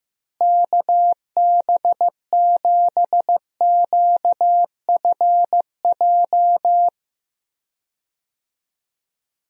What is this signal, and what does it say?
Morse code, heard as a keyed tone, about 15 wpm, 700 Hz. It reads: KB7QFJ
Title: USA Call Signs - 6 Characters - Repeated 2x - 15wpm